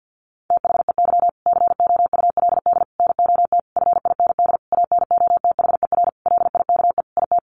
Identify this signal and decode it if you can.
Morse code, heard as a keyed tone, about 50 wpm, 700 Hz. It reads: THEY COULD NOT FIND ANOTHER LIKE IT
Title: Sentences from Top 1400 Words 50wpm